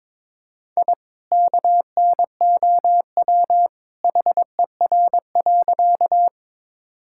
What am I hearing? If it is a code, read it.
Morse code, heard as a keyed tone, about 22 wpm, 700 Hz. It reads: I KNOW HER.
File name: practice / 22wpm / Sentences with 2-Word Phrases - 22wpm